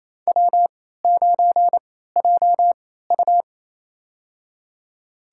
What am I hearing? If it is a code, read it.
Morse code, heard as a keyed tone, about 28 wpm, spaced as for 16 wpm, 700 Hz. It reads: W9JU
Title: USA Call Signs - 4 Characters - 3x Character Spacing - 28wpm